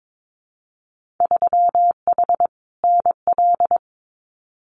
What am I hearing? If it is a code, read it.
Morse code, heard as a keyed tone, about 22 wpm, 700 Hz. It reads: 3H NL
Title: Winter Field Day Exchanges Encore 22wpm